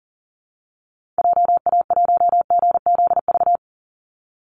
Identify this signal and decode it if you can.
Morse code, heard as a keyed tone, about 39 wpm, 700 Hz. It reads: JA1GZV
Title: Call Signs Repeated 2x - 40wpm